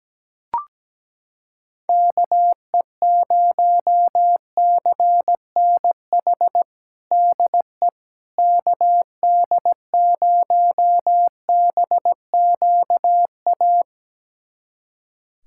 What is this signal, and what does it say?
Morse code, heard as a keyed tone, about 17 wpm, 700 Hz. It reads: KE0CNH DE KD0BQA
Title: USA Callsign DE Callsign - 6 Characters - 17wpm